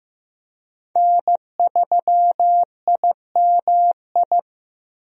Morse code, heard as a keyed tone, about 15 wpm, 700 Hz. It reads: N3IMI
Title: USA Call Signs - 5 Characters - 15wpm